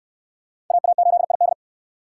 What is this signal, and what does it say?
Morse code, heard as a keyed tone, about 65 wpm, 700 Hz. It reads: AA8ID